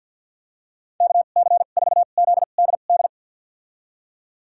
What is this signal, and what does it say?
Morse code, heard as a keyed tone, about 53 wpm, spaced as for 37 wpm, 700 Hz. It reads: KC4ZDD
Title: USA Call Signs - 2x Character Spacing - 50wpm